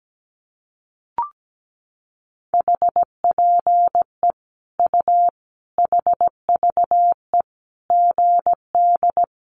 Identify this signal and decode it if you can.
Morse code, heard as a keyed tone, about 17 wpm, 700 Hz. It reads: HPE U HVE GD
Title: Common QSO Phrases 17wpm